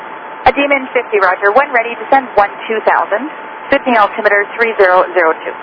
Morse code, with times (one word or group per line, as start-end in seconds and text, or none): none